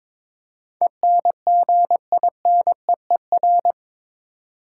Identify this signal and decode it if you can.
Morse code, heard as a keyed tone, about 22 wpm, 700 Hz. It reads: ENGINEER
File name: practice / 22wpm / Eight Letter Words - 22wpm